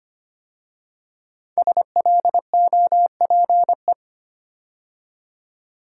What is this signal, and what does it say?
Morse code, heard as a keyed tone, about 25 wpm, 700 Hz. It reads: SLOPE
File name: 5 Letter Words - First 3 Letters Given - 25wpm